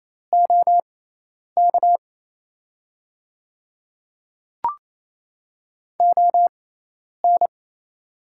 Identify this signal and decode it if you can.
Morse code, heard as a keyed tone, about 28 wpm, spaced as for 10 wpm, 700 Hz. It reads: OK ON
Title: 6x Character Spacing - Top 100 Words - 28wpm